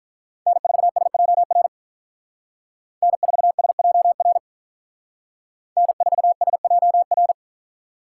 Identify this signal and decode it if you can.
Morse code, heard as a keyed tone, about 46 wpm, 700 Hz. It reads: N4SJR N4SJR N4SJR